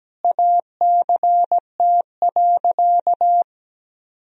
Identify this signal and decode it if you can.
Morse code, heard as a keyed tone, about 17 wpm, 700 Hz. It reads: ACT.
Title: Sentences from Top 1000 Words - Review Part 1 - 17wpm